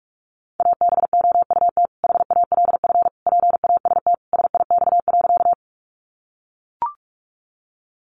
Morse code, heard as a keyed tone, about 44 wpm, 700 Hz. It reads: ABOUT HALF PAST SIX.
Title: Sentences from Top 600 Words 45wpm